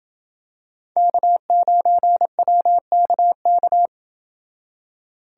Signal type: Morse code

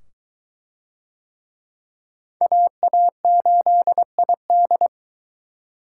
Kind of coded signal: Morse code

